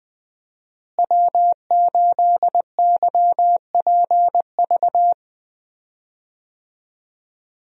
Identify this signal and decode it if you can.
Morse code, heard as a keyed tone, about 20 wpm, 700 Hz. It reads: W8YPV
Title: USA Call Signs - 5 Characters - Repeated 2x - 20wpm